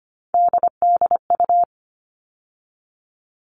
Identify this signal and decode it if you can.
Morse code, heard as a keyed tone, about 25 wpm, 700 Hz. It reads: DDU
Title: All Cut Numbers - 3-Digit 25wpm